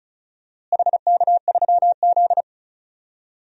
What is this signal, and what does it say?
Morse code, heard as a keyed tone, about 36 wpm, 700 Hz. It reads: HK3Z
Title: DXCC-100 Call Signs - 4 Characters - Repeated 3x - 35wpm